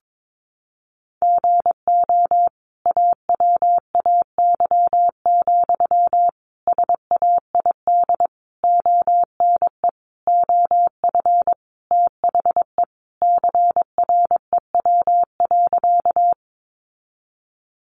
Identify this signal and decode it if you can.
Morse code, heard as a keyed tone, about 22 wpm, 700 Hz. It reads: GO AWAY, SAID ONE OF THE CREW.
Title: Sentences from Top 1500 Words 22wpm